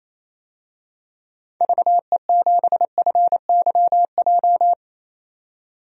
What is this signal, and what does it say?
Morse code, heard as a keyed tone, about 28 wpm, 700 Hz. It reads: VE7FYJ
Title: DXCC-100 Call Signs - 6 Characters - 28wpm